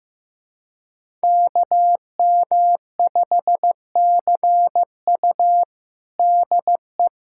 Morse code, heard as a keyed tone, about 15 wpm, 700 Hz. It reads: KM5CU DE
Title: USA Callsign DE Callsign - 5 Characters - 15wpm